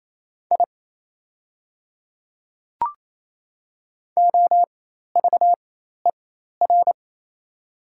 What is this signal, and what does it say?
Morse code, heard as a keyed tone, about 28 wpm, spaced as for 13 wpm, 690 Hz. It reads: I OVER
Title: 4x Character Spacing - Top 200 Words - 28wpm